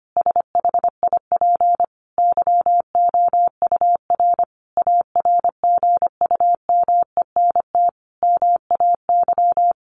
Morse code, heard as a keyed tone, about 25 wpm, 700 Hz. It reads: SHIP YOUR ARGUMENT MAY
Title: Sentences from Top 1400 Words 25wpm